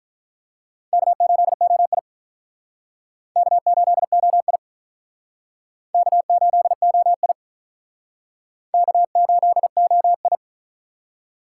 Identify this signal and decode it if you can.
Morse code, heard as a keyed tone, about 45 wpm, 700 Hz. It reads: K8OI K8OI K8OI K8OI